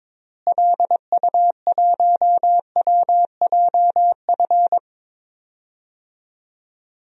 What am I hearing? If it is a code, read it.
Morse code, heard as a keyed tone, about 22 wpm, 700 Hz. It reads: LU1WJF
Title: DXCC-100 Call Signs - 6 Characters - Repeated 2x - 22wpm